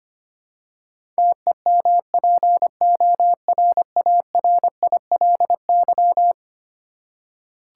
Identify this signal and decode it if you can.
Morse code, heard as a keyed tone, about 25 wpm, 700 Hz. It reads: TEMPORARILY
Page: morse-code-ninja-practice-25wpm